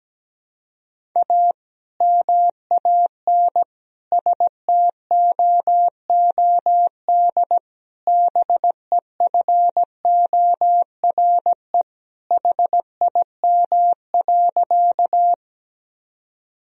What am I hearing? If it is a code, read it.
Morse code, heard as a keyed tone, about 17 wpm, 700 Hz. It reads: A MAN STOOD BEFORE HIM.
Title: Sentences from Top 1000 Words - Review Part 2 - 17wpm